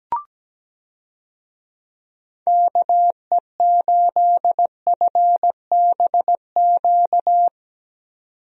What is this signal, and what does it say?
Morse code, heard as a keyed tone, about 17 wpm, 700 Hz. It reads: KE8FBQ
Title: USA Call Signs - 6 Characters - 17wpm